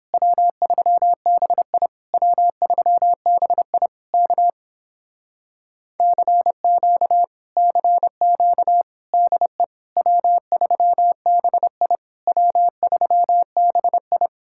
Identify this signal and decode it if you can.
Morse code, heard as a keyed tone, about 29 wpm, 700 Hz. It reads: W3BI W3BI K CQ CQ DE W3BI W3BI